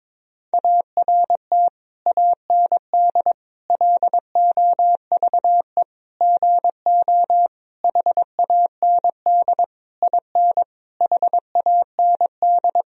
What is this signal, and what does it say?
Morse code, heard as a keyed tone, about 22 wpm, 700 Hz. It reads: ART AND LOVE GO HAND IN HAND
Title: Sentences from Top 1000 Words - Review Part 3 - 22wpm